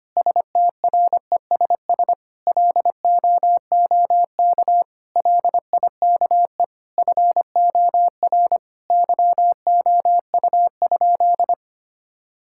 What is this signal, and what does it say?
Morse code, heard as a keyed tone, about 25 wpm, 700 Hz. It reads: STRESS LOOK LIKE FOR YOU?
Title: Sentences from Top 1000 Words - Review Part 6 - 25wpm